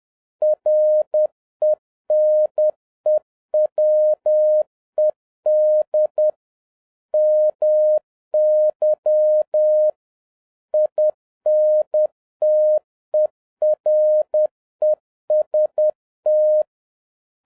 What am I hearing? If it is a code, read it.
Morse code, heard as a keyed tone, about 10 wpm, 610 Hz. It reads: RENEWED MY INTEREST